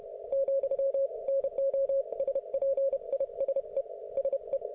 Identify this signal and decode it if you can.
Morse code, heard as a keyed tone, about 31 wpm, 530 Hz. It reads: , NO SPISE SE